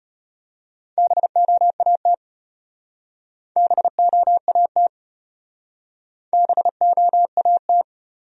Morse code, heard as a keyed tone, about 33 wpm, 700 Hz. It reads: BOAT BOAT BOAT